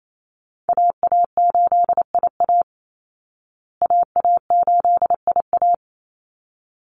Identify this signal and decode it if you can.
Morse code, heard as a keyed tone, about 28 wpm, 700 Hz. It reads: AA8IA AA8IA
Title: Call Signs Repeated 3x - 28wpm